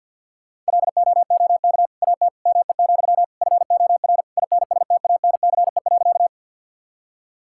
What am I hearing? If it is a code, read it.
Morse code, heard as a keyed tone, about 50 wpm, 690 Hz. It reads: LOOK AT ME, FOR INSTANCE.
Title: Sentences from Top 1000 Words - Review Part 5 - 50wpm